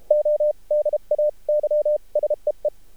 Morse code, heard as a keyed tone, about 33 wpm, spaced as for 28 wpm, 600 Hz. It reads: ODAY SEE
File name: archived / CRSnet